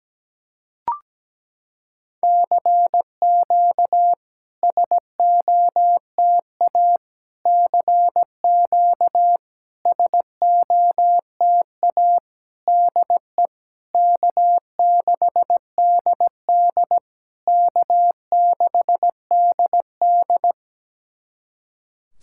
Morse code, heard as a keyed tone, about 17 wpm, 700 Hz. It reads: CQ SOTA CQ SOTA DE K6DD K6DD
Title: SOTA QSO Practice 17wpm